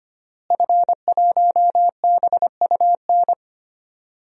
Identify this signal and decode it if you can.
Morse code, heard as a keyed tone, about 25 wpm, 700 Hz. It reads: F1BUN
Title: DXCC-100 Call Signs - 5 Characters - Repeated 3x - 25wpm